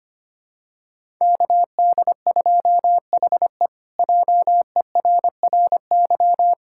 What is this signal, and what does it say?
Morse code, heard as a keyed tone, about 25 wpm, 700 Hz. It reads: KD2HE JERRY
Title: K1USN SST QSO Practice - 25wpm